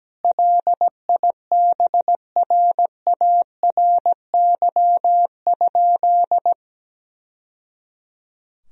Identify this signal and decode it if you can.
Morse code, heard as a keyed tone, about 17 wpm, 700 Hz. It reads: LIBRARY?
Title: Sentences from Top 1000 Words - Review Part 4 - 17wpm